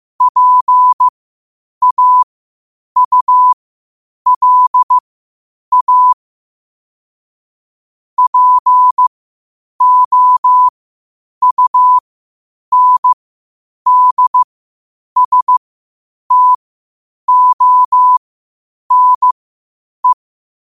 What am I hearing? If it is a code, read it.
Morse code, heard as a keyed tone, about 15 wpm, spaced as for 8 wpm, 1000 Hz. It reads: PAULA POUNDSTONE
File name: qotdfiles / 8wpm